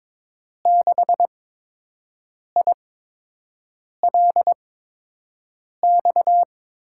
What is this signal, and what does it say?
Morse code, heard as a keyed tone, about 22 wpm, spaced as for 6 wpm, 700 Hz. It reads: 6ILX
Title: USA Call Signs - 6 Characters - 8x Character Spacing - 22wpm